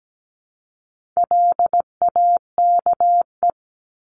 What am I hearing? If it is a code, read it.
Morse code, heard as a keyed tone, about 17 wpm, 700 Hz. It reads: LAKE
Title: Top 1001-1250 Words 17wpm